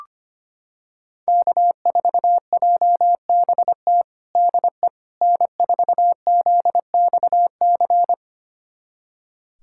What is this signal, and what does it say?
Morse code, heard as a keyed tone, about 25 wpm, 700 Hz. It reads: K4JBT DE N4ZXC
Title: USA Callsign DE Callsign - 5 Characters - 25wpm